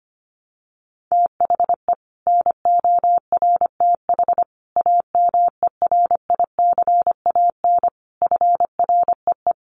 Morse code, heard as a keyed tone, about 25 wpm, 700 Hz. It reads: THE NORTH AMERICAN FREE
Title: Sets of 5 Words Encore 25wpm